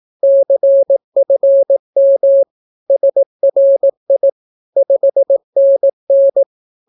Morse code, heard as a keyed tone, about 18 wpm, 550 Hz. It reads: CFM SRI 5NN
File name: naukapodstaw / skroty / skroty